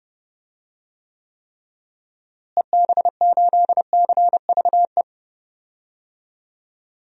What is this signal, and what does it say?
Morse code, heard as a keyed tone, about 30 wpm, 700 Hz. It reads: EB8CVE
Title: DXCC-100 Call Signs - 6 Characters - Repeated 3x - 30wpm